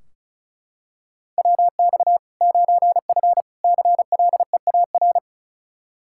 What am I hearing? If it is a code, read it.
Morse code, heard as a keyed tone, about 34 wpm, 700 Hz. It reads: WX 9F CLEAR
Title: K1USN SST QSO Practice 35wpm